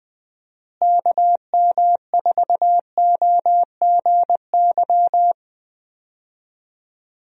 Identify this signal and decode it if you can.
Morse code, heard as a keyed tone, about 20 wpm, 700 Hz. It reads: KM4OGY
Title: USA Call Signs - 6 Characters - Repeated 3x - 20wpm